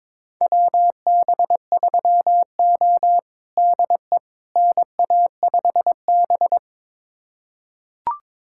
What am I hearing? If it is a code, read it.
Morse code, heard as a keyed tone, about 22 wpm, 700 Hz. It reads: WB3O DE NA5B
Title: USA Callsign DE Callsign - 4 Characters - 22wpm